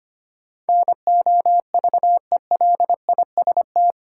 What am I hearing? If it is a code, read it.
Morse code, heard as a keyed tone, about 25 wpm, 700 Hz. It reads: NOVELIST
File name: 8 Letter Words - First 3 Letters Given - 25wpm